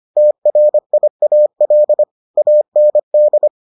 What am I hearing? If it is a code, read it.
Morse code, heard as a keyed tone, about 25 wpm, 600 Hz. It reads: TRIAL AND